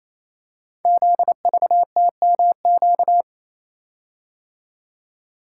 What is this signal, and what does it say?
Morse code, heard as a keyed tone, about 28 wpm, 700 Hz. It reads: ZVTMQ
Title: Code Groups - 5 Letters - 28wpm